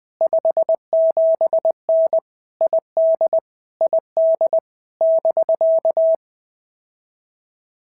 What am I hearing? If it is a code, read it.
Morse code, heard as a keyed tone, about 20 wpm, 650 Hz. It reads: 57N ID ID <BK>
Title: POTA QSO Practice - 20wpm